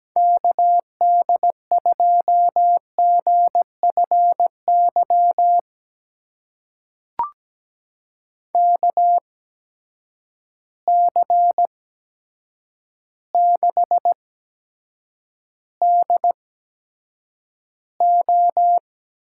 Morse code, heard as a keyed tone, about 17 wpm, spaced as for 8 wpm, 700 Hz. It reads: KD2GFY KC6DO
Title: USA Call Signs - 6 Characters - 8x Character Spacing - 17wpm